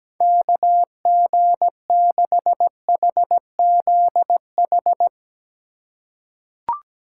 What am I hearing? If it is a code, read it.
Morse code, heard as a keyed tone, about 17 wpm, 700 Hz. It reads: KG6HZH
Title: USA Call Signs - 6 Characters - 4x Character Spacing - 17wpm